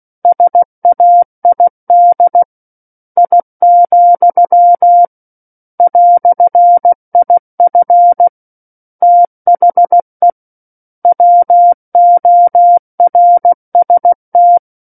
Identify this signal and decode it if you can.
Morse code, heard as a keyed tone, about 16 wpm, 700 Hz. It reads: SAID I, "IF THE WORST
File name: Book One / 07-Chapter 7 - 16 WPM